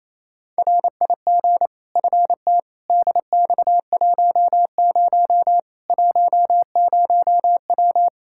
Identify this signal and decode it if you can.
Morse code, heard as a keyed tone, about 28 wpm, 700 Hz. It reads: RIG FT DX10 10W
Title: K1USN SST QSO Practice 28wpm